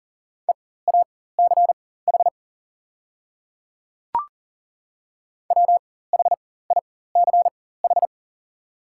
Morse code, heard as a keyed tone, about 38 wpm, spaced as for 18 wpm, 690 Hz. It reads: EACH WHICH
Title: 4x Character Spacing - Top 100 Words - 40wpm